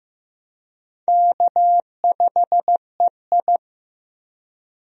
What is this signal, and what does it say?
Morse code, heard as a keyed tone, about 15 wpm, 700 Hz. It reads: K5EI
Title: USA Call Signs - 4 Characters - Repeated 3x - 15wpm